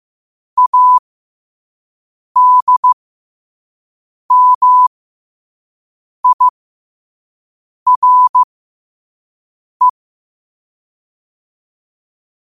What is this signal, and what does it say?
Morse code, heard as a keyed tone, about 15 wpm, spaced as for 5 wpm, 1000 Hz. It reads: ADMIRE
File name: qotdfiles / 5wpm